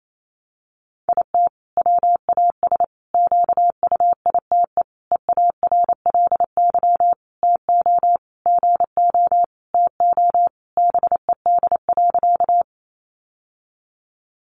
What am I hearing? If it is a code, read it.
Morse code, heard as a keyed tone, about 28 wpm, 700 Hz. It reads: IT WAS QUITE EARLY TO GO TO BED.
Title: Sentences from Top 700 Words 28wpm